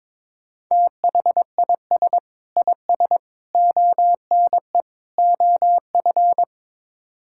Morse code, heard as a keyed tone, about 22 wpm, 700 Hz. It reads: THIS IS ONE OF
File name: practice / 22wpm / Build A Phrase - 5-Word Phrases - 22wpm